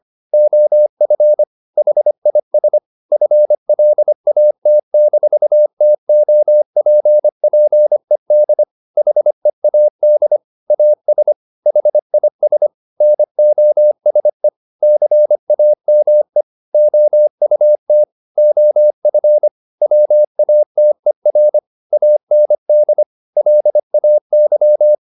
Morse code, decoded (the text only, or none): OF HIS FLAT-TOPPED HEAD AS HIS NOSE CAME OUT OF WATER AND LAY